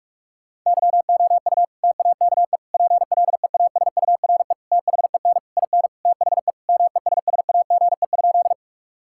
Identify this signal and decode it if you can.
Morse code, heard as a keyed tone, about 41 wpm, 700 Hz. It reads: YOU TAKE PLEASURE THEN IN THE MESSAGE?
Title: Sentences from Top 1600 Words 45wpm